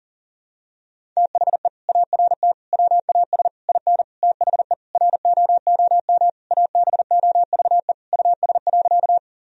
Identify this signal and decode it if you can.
Morse code, heard as a keyed tone, about 40 wpm, 700 Hz. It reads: THE ART WAS IN THE ROOM ABOVE US.
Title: Sentences from Top 1000 Words - Review Part 3 - 40wpm